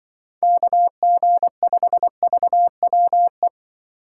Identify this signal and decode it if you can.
Morse code, heard as a keyed tone, about 23 wpm, 700 Hz. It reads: KG5VWE
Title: Speed Racing - USA Call Signs 20wpm